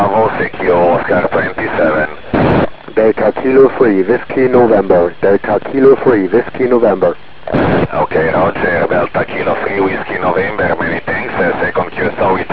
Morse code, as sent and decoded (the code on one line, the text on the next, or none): none